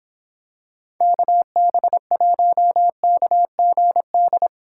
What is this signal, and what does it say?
Morse code, heard as a keyed tone, about 26 wpm, 700 Hz. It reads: KB1KGD